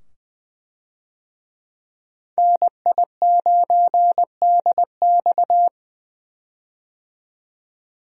Morse code, heard as a keyed tone, about 20 wpm, 700 Hz. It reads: NI9DX
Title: USA Call Signs - 5 Characters - 20wpm